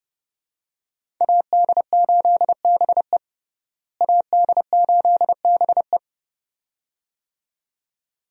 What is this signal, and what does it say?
Morse code, heard as a keyed tone, about 30 wpm, 700 Hz. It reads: AD8BE AD8BE
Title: USA Call Signs - 5 Characters - Repeated 2x - 30wpm